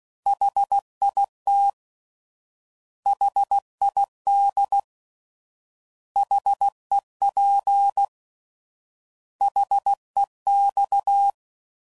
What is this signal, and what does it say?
Morse code, heard as a keyed tone, about 16 wpm, 800 Hz. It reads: HIT HID HEP HEX
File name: Prefixes / By First Letter / Hs